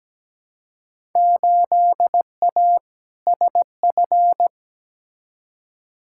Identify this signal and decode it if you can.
Morse code, heard as a keyed tone, about 17 wpm, 700 Hz. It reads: 8A SF
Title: Field Day Exchanges Encore - 17wpm